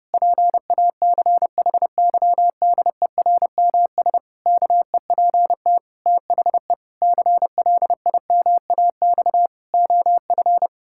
Morse code, heard as a keyed tone, about 32 wpm, 700 Hz. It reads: PACHYDERMS KEPT THE CLIMAX OF
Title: Pangram Challenge v2 - 30wpm